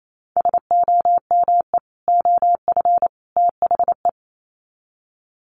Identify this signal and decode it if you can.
Morse code, heard as a keyed tone, about 28 wpm, 700 Hz. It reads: SOME OF THE
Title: Sets of 3 Words 28wpm